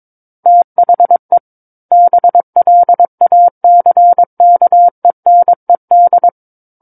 Morse code, heard as a keyed tone, about 22 wpm, 700 Hz. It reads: THE BLACKENED
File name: Book One / 13-Chapter 13 - 22 WPM